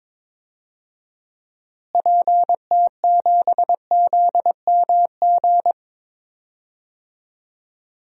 Morse code, heard as a keyed tone, about 22 wpm, 700 Hz. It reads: PT7ZMG
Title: DXCC-100 Call Signs - 6 Characters - Repeated 2x - 22wpm